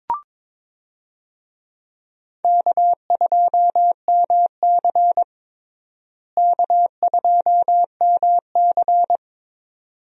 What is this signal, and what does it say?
Morse code, heard as a keyed tone, about 22 wpm, 700 Hz. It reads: K2MC K2MC
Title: USA Call Signs - 4 Characters - Repeated 2x - 22wpm